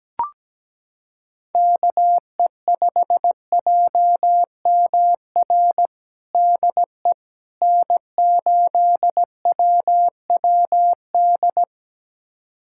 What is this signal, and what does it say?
Morse code, heard as a keyed tone, about 17 wpm, 700 Hz. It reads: KE5JMR DE N8WWD
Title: USA Callsign DE Callsign 17wpm